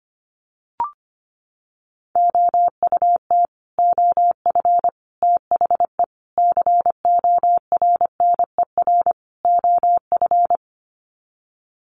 Morse code, heard as a keyed tone, about 25 wpm, 700 Hz. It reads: OUT OF THE CORNER OF
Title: Sets of 5 Words 25wpm